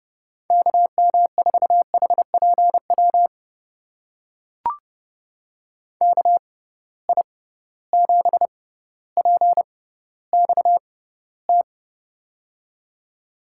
Morse code, heard as a keyed tone, about 32 wpm, spaced as for 16 wpm, 700 Hz. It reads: KM4HPW KI7PXT
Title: USA Call Signs - 6 Characters - 6x Character Spacing - 30wpm